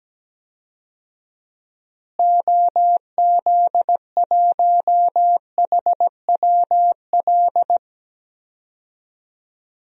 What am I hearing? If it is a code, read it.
Morse code, heard as a keyed tone, about 17 wpm, 700 Hz. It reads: OZ1HWL
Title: DXCC-100 Call Signs - 6 Characters - Repeated 3x - 17wpm